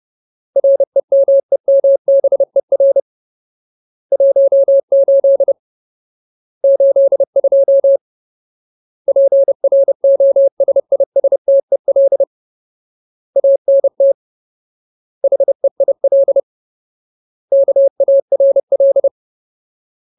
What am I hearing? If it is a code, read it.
Morse code, heard as a keyed tone, about 30 wpm, 550 Hz. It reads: REMEMBER 18 82 PROSISTEL ANT HEIL KARL
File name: Ultimate Word List / 30WPM Ultimate